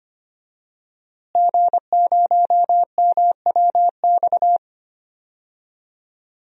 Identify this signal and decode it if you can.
Morse code, heard as a keyed tone, about 25 wpm, 700 Hz. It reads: G0MWX
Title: DXCC-100 Call Signs - 5 Characters - Repeated 2x - 25wpm